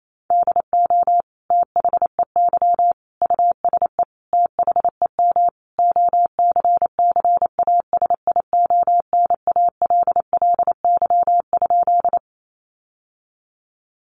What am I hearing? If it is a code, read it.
Morse code, heard as a keyed tone, about 28 wpm, 700 Hz. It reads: DO THEY USE THEM OCCASIONALLY?
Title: Sentences from Top 1300 Words 28wpm